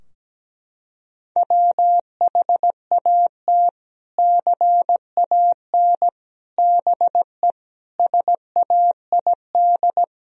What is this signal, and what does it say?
Morse code, heard as a keyed tone, about 17 wpm, 700 Hz. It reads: WHAT CAN BE SAID
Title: Sentences from Top 1000 Words - Review Part 4 - 17wpm